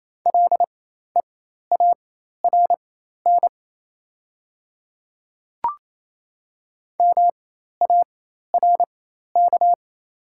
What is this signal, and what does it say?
Morse code, heard as a keyed tone, about 28 wpm, spaced as for 13 wpm, 700 Hz. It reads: LEARN MARK
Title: 4x Character Spacing - Top 300 Words - 28wpm